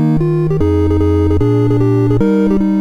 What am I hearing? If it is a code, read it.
There is no code.